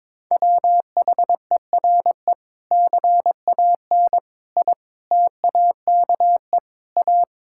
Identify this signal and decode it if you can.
Morse code, heard as a keyed tone, about 22 wpm, 700 Hz. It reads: WHERE CAN I TAKE A
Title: Sentences from Top 1600 Words 22wpm